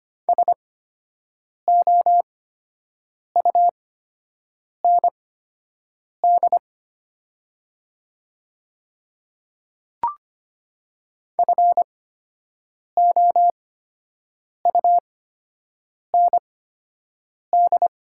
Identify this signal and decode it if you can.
Morse code, heard as a keyed tone, about 25 wpm, spaced as for 7 wpm, 700 Hz. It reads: SOUND FOUND
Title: 8x Character Spacing - Top 500 Words - 25wpm